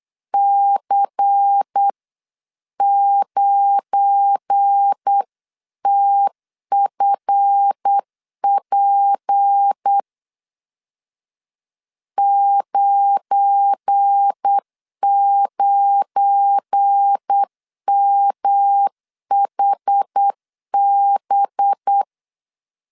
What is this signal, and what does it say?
Morse code, heard as a keyed tone, about 8 wpm, 800 Hz. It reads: C9TFP 99MHB
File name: LEZIONI / 7 GF9Z / 7 GF9 8wpm 3545